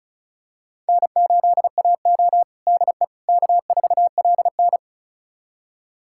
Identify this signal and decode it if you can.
Morse code, heard as a keyed tone, about 34 wpm, 700 Hz. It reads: N8AO DE K4LN
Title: USA Callsign DE Callsign - 4 Characters - 35wpm